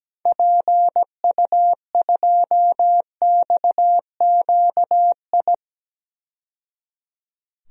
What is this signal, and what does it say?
Morse code, heard as a keyed tone, about 17 wpm, 700 Hz. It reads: PU2XQI
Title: DXCC-100 Call Signs - 6 Characters - 17wpm